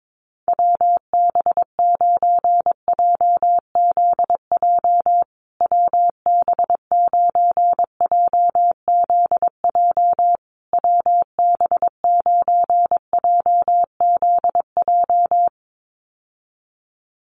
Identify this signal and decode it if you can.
Morse code, heard as a keyed tone, about 22 wpm, 700 Hz. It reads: WB9JZJ WB9JZJ WB9JZJ
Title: USA Call Signs Repeated 3x 22wpm